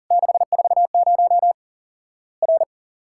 Morse code, heard as a keyed tone, about 40 wpm, 700 Hz. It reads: 640 R
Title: Worked All Europe DX Contest - QTC Exchange - 40wpm